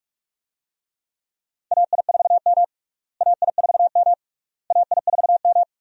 Morse code, heard as a keyed tone, about 46 wpm, 700 Hz. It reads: AI4M AI4M AI4M
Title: USA Call Signs - 4 Characters - Repeated 3x - 45wpm